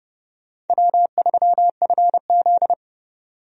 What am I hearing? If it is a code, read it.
Morse code, heard as a keyed tone, about 30 wpm, 700 Hz. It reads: W3FZ
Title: USA Call Signs - 4 Characters - 30wpm